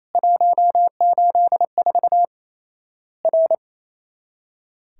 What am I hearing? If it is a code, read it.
Morse code, heard as a keyed tone, about 28 wpm, 700 Hz. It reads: 184 R
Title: Worked All Europe DX Contest - QTC Exchange - 28wpm